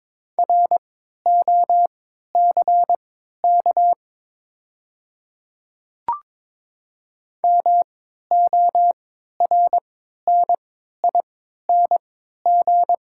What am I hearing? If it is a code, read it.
Morse code, heard as a keyed tone, about 22 wpm, spaced as for 13 wpm, 700 Hz. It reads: ROCK MORNING
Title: 3x Character Spacing - Top 500 Words - 22wpm